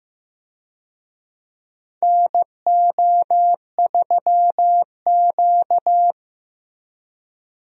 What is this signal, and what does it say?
Morse code, heard as a keyed tone, about 15 wpm, 700 Hz. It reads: NO3Q
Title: USA Call Signs - 4 Characters - 15wpm